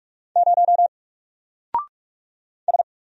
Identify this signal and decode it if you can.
Morse code, heard as a keyed tone, about 44 wpm, 700 Hz. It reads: 0 S
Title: Single Character (Letter:Number:Punctuation:Prosign) 45wpm